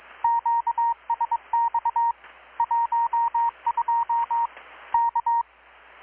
Morse code, heard as a keyed tone, about 22 wpm, 930 Hz. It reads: QSX 12 K
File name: blog / swl / 6-abril-2019-12695-USB-KPS